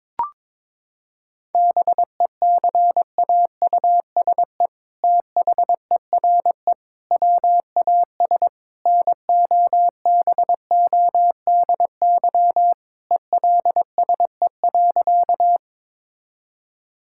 Morse code, heard as a keyed tone, about 22 wpm, 700 Hz. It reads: BECAUSE THERE WAS NOBODY ELSE.
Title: Sentences from Top 1600 Words 22wpm